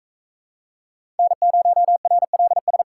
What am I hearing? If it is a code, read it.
Morse code, heard as a keyed tone, about 38 wpm, 700 Hz. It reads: N0RLS